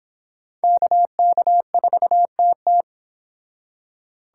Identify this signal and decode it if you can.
Morse code, heard as a keyed tone, about 25 wpm, 700 Hz. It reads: KK4TT